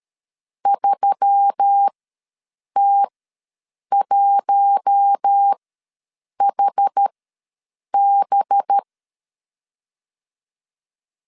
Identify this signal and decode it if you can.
Morse code, heard as a keyed tone, about 13 wpm, spaced as for 7 wpm, 800 Hz. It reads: 3T1HB
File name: LEZIONI / 5 LBVC / 5 LBV ws 12wpm 3015